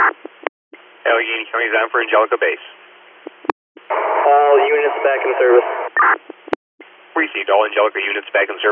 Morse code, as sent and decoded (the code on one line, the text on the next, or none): none